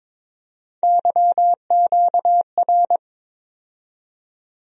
Morse code, heard as a keyed tone, about 22 wpm, 700 Hz. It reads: YQR
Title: USA Call Sign Postfixes - 3 Letters 22wpm